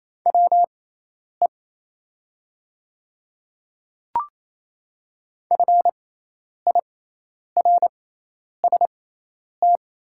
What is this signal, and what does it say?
Morse code, heard as a keyed tone, about 28 wpm, spaced as for 10 wpm, 700 Hz. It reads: WE FIRST
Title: 6x Character Spacing - Top 100 Words - 28wpm